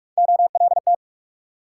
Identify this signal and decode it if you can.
Morse code, heard as a keyed tone, about 60 wpm, spaced as for 53 wpm, 700 Hz. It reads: OLT
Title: Speed-Racing Multi-letter Phonemes - 45wpm